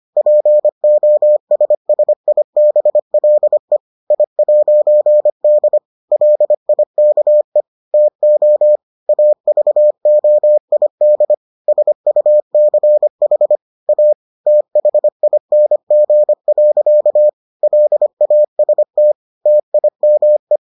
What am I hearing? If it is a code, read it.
Morse code, heard as a keyed tone, about 25 wpm, 600 Hz. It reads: POSSIBLE I'D LIKE TO AVOID SUCH A THING. LAST TIME